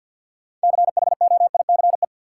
Morse code, heard as a keyed tone, about 52 wpm, 700 Hz. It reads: CHOICE
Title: Sentences from Top 900 Words - New Word Drill - 50wpm